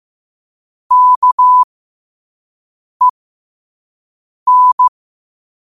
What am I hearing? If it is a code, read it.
Morse code, heard as a keyed tone, about 15 wpm, spaced as for 5 wpm, 1000 Hz. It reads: KEN
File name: qotdfiles / 5wpm